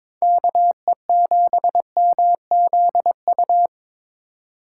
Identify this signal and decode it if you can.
Morse code, heard as a keyed tone, about 26 wpm, 700 Hz. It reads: KE7MZU
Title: Speed Racing - USA 2x3 Call Signs 22wpm